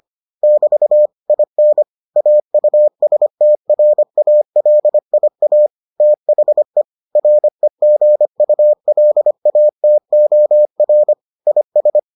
Morse code, heard as a keyed tone, about 25 wpm, 600 Hz. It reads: <BT> IN AUSTRALIA THE REGULATOR IS